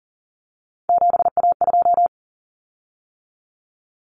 Morse code, heard as a keyed tone, about 40 wpm, 700 Hz. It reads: 7A2
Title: Common Cut Numbers - 3-Digit 40wpm